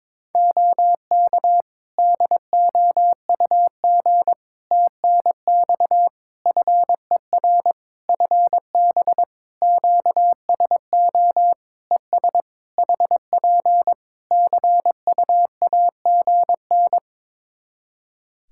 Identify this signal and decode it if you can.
Morse code, heard as a keyed tone, about 22 wpm, 700 Hz. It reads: OK DOUG TNX FER FB QSO ES HP CUAGN
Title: Practice QSO - Exchange 4 - Ending - 22wpm